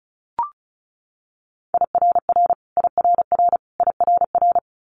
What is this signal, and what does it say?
Morse code, heard as a keyed tone, about 36 wpm, 700 Hz. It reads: IRR IRR IRR
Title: Multi-letter Phonemes Repeated 3x 35wpm